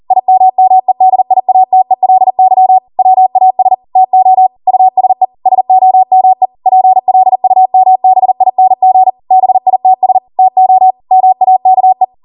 Morse code, heard as a keyed tone, about 40 wpm, 750 Hz. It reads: IMMEDIATELY WAS TO USE SOME PLUMBING BITS TO MAKE
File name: Morse / 40WPM